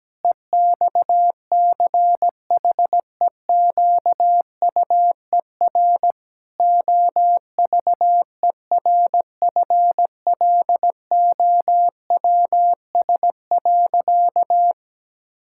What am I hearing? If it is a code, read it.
Morse code, heard as a keyed tone, about 17 wpm, 700 Hz. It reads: EXCHEQUER OVERFLOWS.
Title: Pangram Challenge v2 - 17wpm